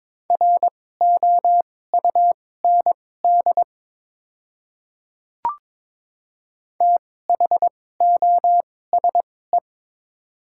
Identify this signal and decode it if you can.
Morse code, heard as a keyed tone, about 21 wpm, spaced as for 16 wpm, 700 Hz. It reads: ROUND THOSE
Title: 2x Character Spacing - Top 300 Words - 22wpm